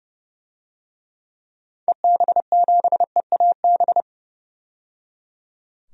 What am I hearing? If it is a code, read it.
Morse code, heard as a keyed tone, about 31 wpm, 700 Hz. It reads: EB7EAB
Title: DXCC-100 Call Signs - 6 Characters - 30wpm